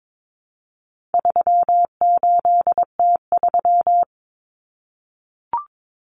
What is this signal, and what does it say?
Morse code, heard as a keyed tone, about 22 wpm, 700 Hz. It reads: 38T3
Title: Common Cut Numbers - 4-Digit 22wpm